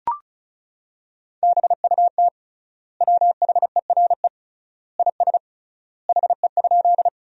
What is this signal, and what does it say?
Morse code, heard as a keyed tone, about 36 wpm, 700 Hz. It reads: BUT WHERE IS HE?
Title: 3x Word-Spacing - Sentences from Top 200 Words - 35wpm